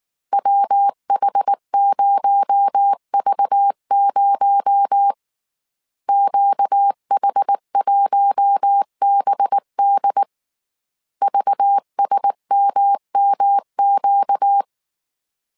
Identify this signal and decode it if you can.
Morse code, heard as a keyed tone, about 19 wpm, 800 Hz. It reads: WH0V0 QH1BD VSMMQ